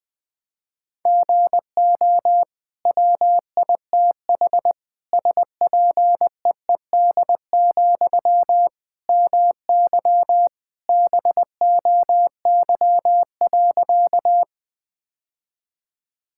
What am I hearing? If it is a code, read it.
Morse code, heard as a keyed tone, about 21 wpm, 700 Hz. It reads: GO WITH SPEED, MY BOY.
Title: Sentences from Top 1000 Words - Review Part 5 - 20wpm